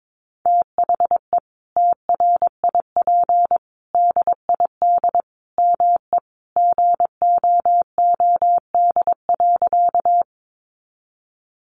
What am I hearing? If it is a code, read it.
Morse code, heard as a keyed tone, about 22 wpm, 700 Hz. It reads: THE TRIP DID ME GOOD.
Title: Sentences from Top 1300 Words 22wpm